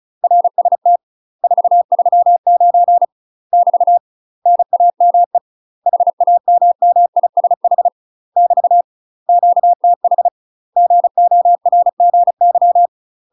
Morse code, heard as a keyed tone, about 36 wpm, 700 Hz. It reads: RST 439 <BT> NAME HAMMISH <BT> QTH GORGY
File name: UK MASTER QSOs TRAINING / 35WPM / UK MASTER QSOs 0009